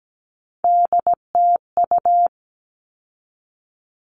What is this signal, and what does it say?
Morse code, heard as a keyed tone, about 17 wpm, 700 Hz. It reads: DTU
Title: All Cut Numbers - 3-Digit 17wpm